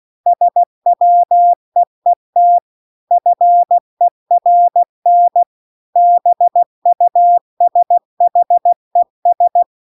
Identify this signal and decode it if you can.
Morse code, heard as a keyed tone, about 16 wpm, 700 Hz. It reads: SWEET FERN BUSHES